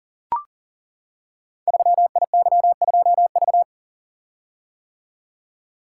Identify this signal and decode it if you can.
Morse code, heard as a keyed tone, about 42 wpm, 700 Hz. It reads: 3IY2V
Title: Code Groups - 5 Letter:Numbers - 40wpm